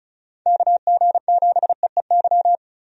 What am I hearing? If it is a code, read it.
Morse code, heard as a keyed tone, about 36 wpm, 700 Hz. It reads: KG7EEY
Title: USA Call Signs - 6 Characters - 35wpm